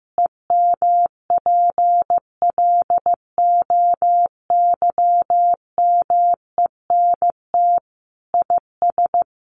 Morse code, heard as a keyed tone, about 15 wpm, 700 Hz. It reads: EMPLOYMENT IS
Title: Sentences from Top 1500 Words 15wpm